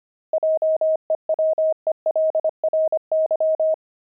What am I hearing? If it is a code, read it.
Morse code, heard as a keyed tone, about 25 wpm, 630 Hz. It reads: JEWELRY
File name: pangram-fs-25-cs-25-deep